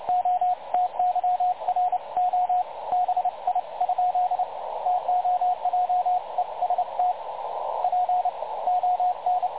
Morse code, heard as a keyed tone, about 29 wpm, spaced as for 24 wpm, 720 Hz, with radio noise. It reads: OTYR OBI ? TOJEST PON
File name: dlazaawansowanych / nagraniazpasma / websdr recording 3565.0kHz